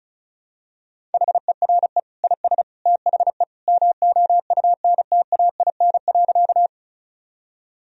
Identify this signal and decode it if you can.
Morse code, heard as a keyed tone, about 36 wpm, 700 Hz. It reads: HERE IS THE MOUNTAIN.
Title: Sentences from Top 1000 Words - Review Part 1 - 35wpm